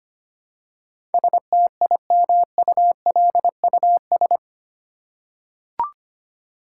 Morse code, heard as a keyed tone, about 25 wpm, 700 Hz. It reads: STIMULUS
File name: 8 Letter Words - First 3 Letters Given - 25wpm